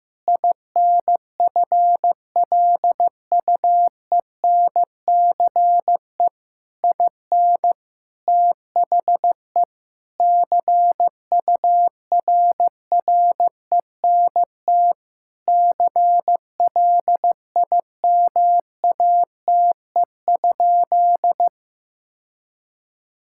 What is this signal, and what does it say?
Morse code, heard as a keyed tone, about 15 wpm, 700 Hz. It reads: INFLUENCE IN THE CURRENT CLIMATE?
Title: Sentences from Top 1600 Words 15wpm